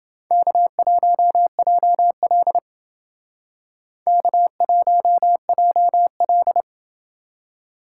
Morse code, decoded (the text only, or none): K1JL K1JL